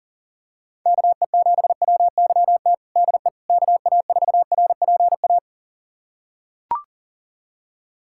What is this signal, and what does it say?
Morse code, heard as a keyed tone, about 37 wpm, 700 Hz. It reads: KE7WYT DE KA4RPA
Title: USA Callsign DE Callsign 40wpm